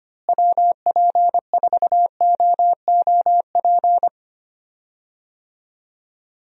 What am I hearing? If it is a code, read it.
Morse code, heard as a keyed tone, about 25 wpm, 700 Hz. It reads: WP4OOP
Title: USA Call Signs - 6 Characters - Repeated 3x - 25wpm